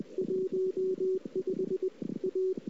Morse code, heard as a keyed tone, about 25 wpm, 380 Hz. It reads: 15 A
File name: CW / 2008-ss-cw-k5tr-op-n5rz -01:00